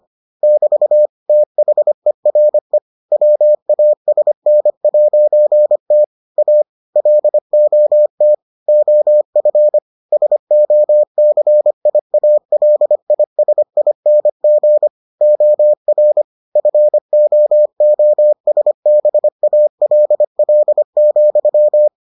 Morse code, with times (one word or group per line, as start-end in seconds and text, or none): 0.4-1.1 <BT>
1.3-2.8 THERE
3.1-6.0 WASN'T
6.4-6.6 A
7.0-8.3 LOT
8.7-9.8 OF
10.1-14.9 SOCIALISING
15.2-16.2 OR
16.6-21.9 FOOSBALL,